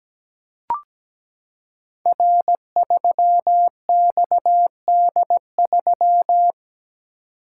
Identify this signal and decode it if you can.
Morse code, heard as a keyed tone, about 17 wpm, 700 Hz. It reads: R3XD3
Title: Code Groups - 5 Letter:Numbers - 17wpm